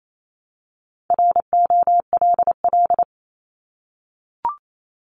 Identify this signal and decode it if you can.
Morse code, heard as a keyed tone, about 28 wpm, 700 Hz. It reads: ROLL
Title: Multi-letter Phonemes Repeated 3x 28wpm